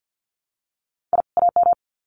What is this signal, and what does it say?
Morse code, heard as a keyed tone, about 50 wpm, 700 Hz. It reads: I AM
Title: Sets of 2 Words 50wpm